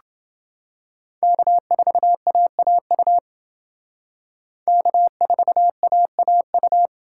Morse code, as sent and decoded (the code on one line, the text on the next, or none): -.- ....- .- .- ..- / -.- ....- .- .- ..-
K4AAU K4AAU